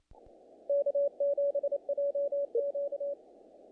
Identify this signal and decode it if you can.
Morse code, heard as a keyed tone, about 27 wpm, 500 Hz. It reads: K7JQ
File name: contests / naqp-cw-k5tr-op-k5pi- -1800